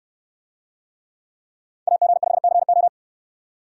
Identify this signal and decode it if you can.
Morse code, heard as a keyed tone, about 69 wpm, 700 Hz. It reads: AG5CO